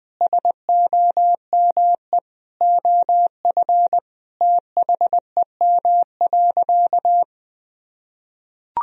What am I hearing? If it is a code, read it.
Morse code, heard as a keyed tone, about 20 wpm, 700 Hz. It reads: SOME OF THEM.
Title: Sentences from Top 1000 Words - Review Part 3 - 20wpm